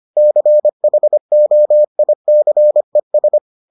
Morse code, heard as a keyed tone, about 25 wpm, 600 Hz. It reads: CHOICES